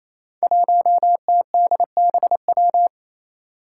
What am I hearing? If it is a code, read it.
Morse code, heard as a keyed tone, about 29 wpm, 700 Hz. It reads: 1TDBW
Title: Five Challenging Letter-Number Combinations 28wpm